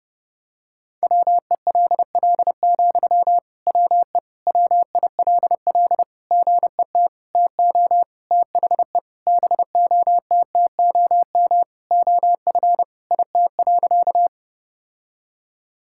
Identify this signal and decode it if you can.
Morse code, heard as a keyed tone, about 28 wpm, 700 Hz. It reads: WELL, WE WILL GET TO THE BOTTOM OF IT.
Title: Sentences from Top 1000 Words - Review Part 6 - 30wpm